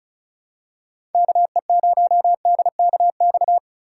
Morse code, heard as a keyed tone, about 36 wpm, 700 Hz. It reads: KE0DKX
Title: USA Call Signs - 6 Characters - 2x Character Spacing - 35wpm